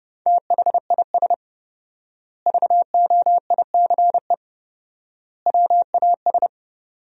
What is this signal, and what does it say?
Morse code, heard as a keyed tone, about 30 wpm, 700 Hz. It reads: THIS VOICE WAS
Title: 4x Word-Spacing - Sentences from Top 500 Words - 30wpm